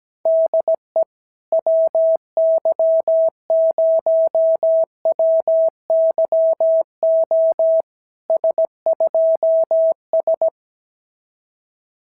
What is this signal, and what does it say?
Morse code, heard as a keyed tone, about 17 wpm, 650 Hz. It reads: DE WY0WYO S2S
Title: SOTA QSO Practice 17wpm